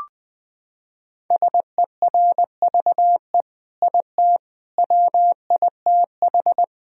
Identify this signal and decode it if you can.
Morse code, heard as a keyed tone, about 21 wpm, 700 Hz. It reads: SERVE IT WITH
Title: Sentences from Top 1000 Words - Review Part 3 - 20wpm